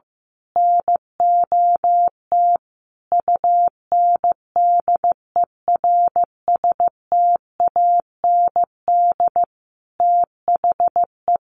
Morse code, heard as a keyed tone, about 15 wpm, 700 Hz. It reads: NOT UNDERSTAND THE
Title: Sentences from Top 600 Words 15wpm